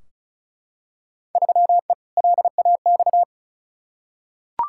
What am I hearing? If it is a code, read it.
Morse code, heard as a keyed tone, about 34 wpm, 700 Hz. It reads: 3E LAX
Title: Field Day Exchanges Encore - 35wpm